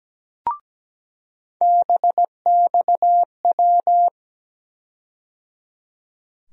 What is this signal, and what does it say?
Morse code, heard as a keyed tone, about 17 wpm, 700 Hz. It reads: BXW
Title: USA Call Sign Postfixes - 3 Letters 17wpm